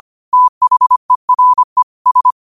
Morse code, heard as a keyed tone, about 25 wpm, 1000 Hz. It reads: THERE S
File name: qotdfiles / 25wpm